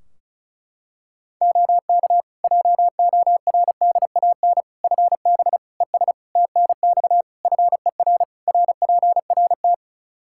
Morse code, heard as a keyed tone, about 36 wpm, 700 Hz. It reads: OK JORDAN FB ES TNX FER RPRT
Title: Practice QSO - Exchange 2 - Info - 35wpm